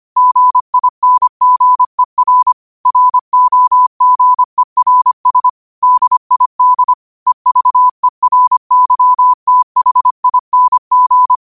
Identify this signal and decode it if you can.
Morse code, heard as a keyed tone, about 25 wpm, 1000 Hz. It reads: GINGER ROGERS DID EVERYTHING
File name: qotdfiles / 25wpm